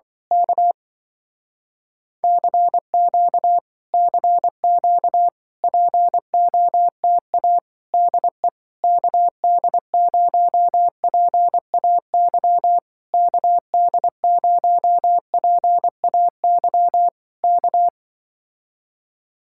Morse code, heard as a keyed tone, about 24 wpm, 700 Hz. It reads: K CQ CQ POTA DE KD0PAY KD0PAY K
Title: Speed-Racing Common QSO Phrases - 20wpm